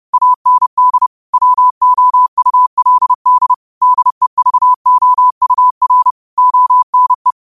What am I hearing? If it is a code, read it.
Morse code, heard as a keyed tone, about 30 wpm, 1000 Hz. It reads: AND WOULD DEVOUR ONE
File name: qotdfiles / 30wpm